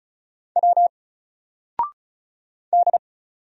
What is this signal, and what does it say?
Morse code, heard as a keyed tone, about 33 wpm, 700 Hz. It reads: W D
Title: Single Letter-Number 35wpm